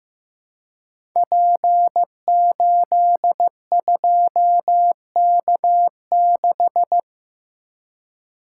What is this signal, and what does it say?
Morse code, heard as a keyed tone, about 15 wpm, 700 Hz. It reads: P82K6
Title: Code Groups - 5 Letter:Numbers - 15wpm